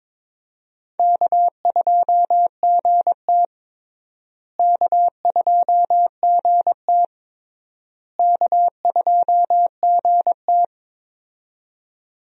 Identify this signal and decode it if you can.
Morse code, heard as a keyed tone, about 21 wpm, 700 Hz. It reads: K2GT K2GT K2GT
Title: USA Call Signs - 4 Characters - Repeated 3x - 22wpm